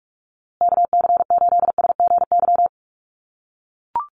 Morse code, heard as a keyed tone, about 44 wpm, 700 Hz. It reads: KC8SGY
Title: USA Call Signs Repeated 2x 45wpm